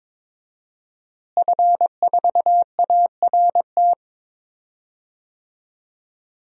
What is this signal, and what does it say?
Morse code, heard as a keyed tone, about 21 wpm, 700 Hz. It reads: F4ART
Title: DXCC-100 Call Signs - 5 Characters - Repeated 2x - 22wpm